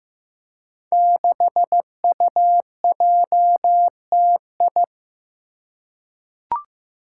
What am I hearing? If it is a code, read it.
Morse code, heard as a keyed tone, about 15 wpm, 700 Hz. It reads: 6UJTI
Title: Five Challenging Letter-Number Combinations 15wpm